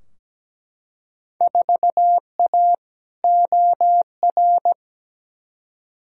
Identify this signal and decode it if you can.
Morse code, heard as a keyed tone, about 17 wpm, 700 Hz. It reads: 4A OR
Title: Field Day Exchanges Encore - 17wpm